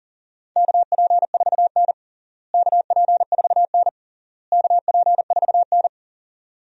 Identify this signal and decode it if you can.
Morse code, heard as a keyed tone, about 40 wpm, 700 Hz. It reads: KP4N KP4N KP4N
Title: DXCC-100 Call Signs - 4 Characters - Repeated 3x - 40wpm